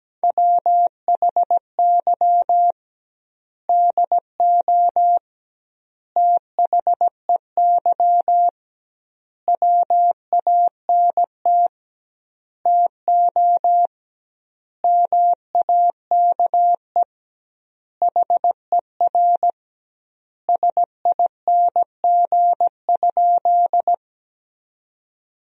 Morse code, heard as a keyed tone, about 17 wpm, 700 Hz. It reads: WHY DO THEY WANT TO MAKE HER SING?
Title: 2x Word-Spacing - Sentences from Top 400 Words - 17wpm